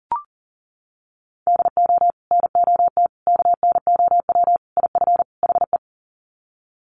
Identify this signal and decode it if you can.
Morse code, heard as a keyed tone, about 40 wpm, 700 Hz. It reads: DO NOT KNOW IF HE
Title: Sets of 5 Words Encore2 40wpm